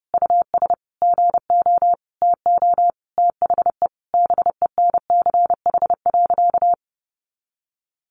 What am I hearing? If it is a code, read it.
Morse code, heard as a keyed tone, about 30 wpm, 700 Hz. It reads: US GO TO THE BENCH.
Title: Sentences from Top 1400 Words 30wpm